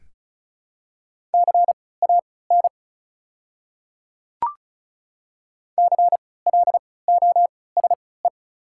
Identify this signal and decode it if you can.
Morse code, heard as a keyed tone, about 34 wpm, spaced as for 20 wpm, 700 Hz. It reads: CAN CLOSE
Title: 3x Character Spacing - Top 300 Words - 35wpm